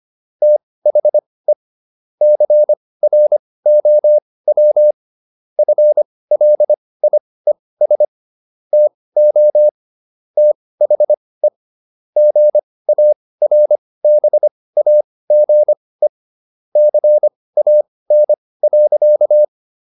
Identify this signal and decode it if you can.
Morse code, heard as a keyed tone, about 25 wpm, spaced as for 18 wpm, 600 Hz. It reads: THE CROW FLIES TO THE GARBAGE CAN.